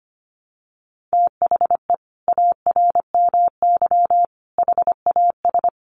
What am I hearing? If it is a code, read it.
Morse code, heard as a keyed tone, about 25 wpm, 700 Hz. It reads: THE ARMY HAS
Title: Sentences from Top 1300 Words 25wpm